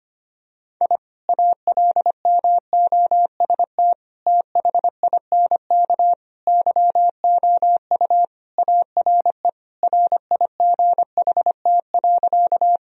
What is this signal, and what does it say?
Morse code, heard as a keyed tone, about 25 wpm, 700 Hz. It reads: I ALMOST THINK YOU ARE RIGHT.
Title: Sentences from Top 1000 Words - Review Part 4 - 25wpm